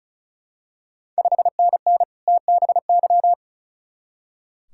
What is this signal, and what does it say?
Morse code, heard as a keyed tone, about 36 wpm, 690 Hz. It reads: 5NN TBY
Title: Ontario QSO Party Exchange 35wpm